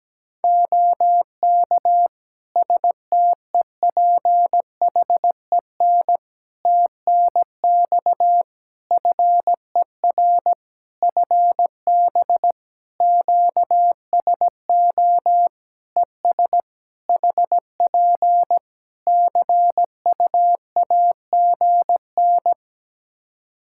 Morse code, heard as a keyed tone, about 17 wpm, 700 Hz. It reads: OK STEPHEN TNX FER FB QSO ES HP CU AGN
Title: Practice QSO - Exchange 4 - Ending - 17wpm